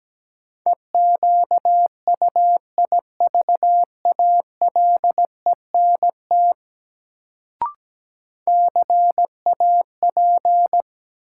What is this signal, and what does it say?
Morse code, heard as a keyed tone, about 17 wpm, 700 Hz. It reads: EQUIVALENT CAP
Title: Sentences from Top 1600 Words - New Word Drill - 17wpm